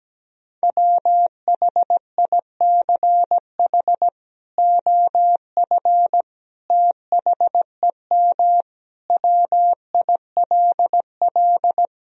Morse code, heard as a keyed tone, about 17 wpm, 700 Hz. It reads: WHICH OF THEM WILL
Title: Sentences from Top 1000 Words - Review Part 3 - 17wpm